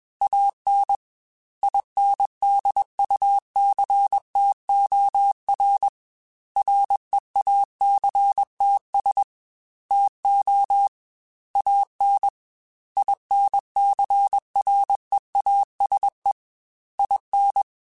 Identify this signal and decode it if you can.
Morse code, heard as a keyed tone, about 21 wpm, 800 Hz. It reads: AN INDUCTOR REACTS TO AN INCREASE IN
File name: Technical Subjects / Technical Subjects-mp3s / XL @20